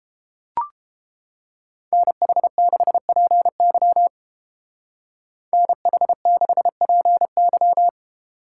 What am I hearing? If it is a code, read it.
Morse code, heard as a keyed tone, about 29 wpm, 700 Hz. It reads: NH6PY NH6PY